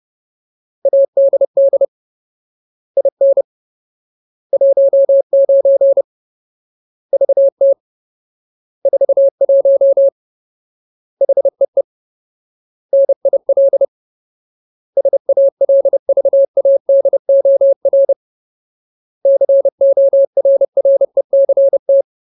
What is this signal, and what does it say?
Morse code, heard as a keyed tone, about 30 wpm, 550 Hz. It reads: ADD IN 19 VT 41 HEE NIL SALVADOR CORRECT
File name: Ultimate Word List / 30WPM Ultimate